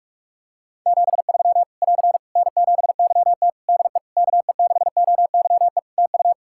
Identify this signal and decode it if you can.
Morse code, heard as a keyed tone, about 41 wpm, 700 Hz. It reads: 73 <AR> N7YT DE KE6OYE TU
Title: Practice QSO - Exchange 4 - Ending - 45wpm